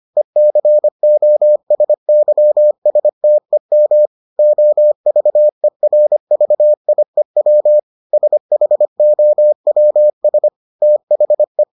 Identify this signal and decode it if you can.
Morse code, heard as a keyed tone, about 25 wpm, 600 Hz. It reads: ECOSYSTEM OVERVIEW SHOWS THE